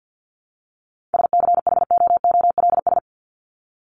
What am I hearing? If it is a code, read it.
Morse code, heard as a keyed tone, about 50 wpm, 700 Hz. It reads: SCHOOLS
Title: ES:S Suffix - Top 100 Words with Base Words 50wpm